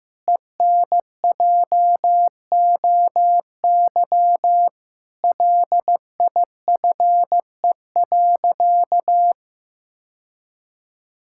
Morse code, heard as a keyed tone, about 15 wpm, 700 Hz. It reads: ENJOY LIFE.
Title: Sentences from Top 1000 Words - Review Part 5 - 15wpm